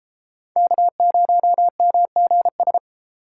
Morse code, heard as a keyed tone, about 35 wpm, 700 Hz. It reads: K0MGS
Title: Speed Racing - USA 1x3 Call Signs 28wpm